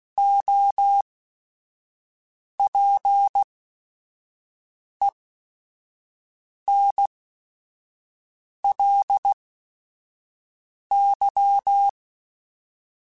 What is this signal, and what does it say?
Morse code, heard as a keyed tone, about 16 wpm, spaced as for 5 wpm, 800 Hz. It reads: OPENLY
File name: Quotes / Quotes-mp3s / Advice @5